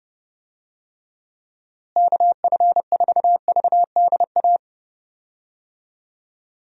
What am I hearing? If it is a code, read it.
Morse code, heard as a keyed tone, about 30 wpm, 700 Hz. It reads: KF4VDA
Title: USA Call Signs - 6 Characters - Repeated 2x - 30wpm